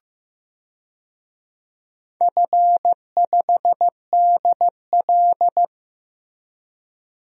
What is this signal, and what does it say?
Morse code, heard as a keyed tone, about 15 wpm, 700 Hz. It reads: F5DL
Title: DXCC-100 Call Signs - 4 Characters - Repeated 3x - 15wpm